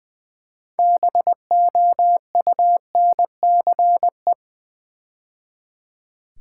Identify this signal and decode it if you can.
Morse code, heard as a keyed tone, about 20 wpm, 690 Hz. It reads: BOUNCE
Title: Build A Word - 3 Letters At A Time - 6-Letter Words - 20wpm